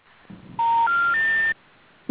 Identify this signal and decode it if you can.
Special information tone.